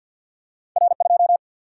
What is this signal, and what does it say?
Morse code, heard as a keyed tone, about 50 wpm, 700 Hz. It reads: R 2
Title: Single-Letter Single-Number 50wpm